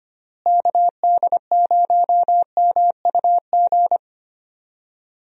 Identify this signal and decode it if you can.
Morse code, heard as a keyed tone, about 25 wpm, 700 Hz. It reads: KD0MUG
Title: USA Call Signs - 6 Characters - Repeated 3x - 25wpm